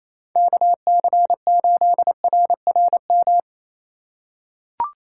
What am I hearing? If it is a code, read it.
Morse code, heard as a keyed tone, about 28 wpm, 700 Hz. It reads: KC8RRM
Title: USA Call Signs - 6 Characters - 2x Character Spacing - 28wpm